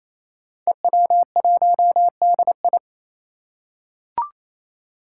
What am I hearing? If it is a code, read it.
Morse code, heard as a keyed tone, about 28 wpm, 700 Hz. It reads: EW1DI
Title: Five Challenging Letter-Number Combinations 28wpm